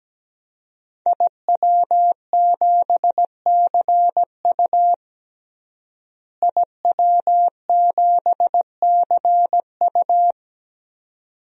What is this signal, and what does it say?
Morse code, heard as a keyed tone, about 17 wpm, 700 Hz. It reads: IW7CU IW7CU
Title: DXCC-100 Call Signs - 5 Characters - Repeated 2x - 17wpm